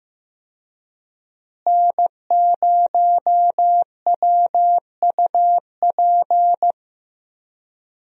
Morse code, heard as a keyed tone, about 15 wpm, 700 Hz. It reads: N0WUP
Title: USA Call Signs - 5 Characters - Repeated 2x - 15wpm